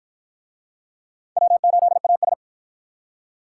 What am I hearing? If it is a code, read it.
Morse code, heard as a keyed tone, about 53 wpm, 700 Hz. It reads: W8AS